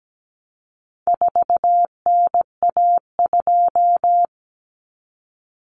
Morse code, heard as a keyed tone, about 17 wpm, 700 Hz. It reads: 4NA2
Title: Common Cut Numbers - 4-Digit 17wpm